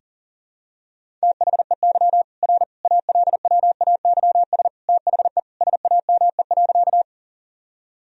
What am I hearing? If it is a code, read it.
Morse code, heard as a keyed tone, about 40 wpm, 700 Hz. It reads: THEY R ALWAYS THE SAME.
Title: Sentences from Top 300 Words and Abbreviations 40wpm